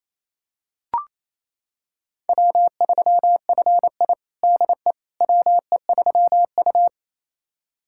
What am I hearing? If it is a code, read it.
Morse code, heard as a keyed tone, about 28 wpm, 700 Hz. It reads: W3FI DE WE3U
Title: USA Callsign DE Callsign - 4 Characters - 28wpm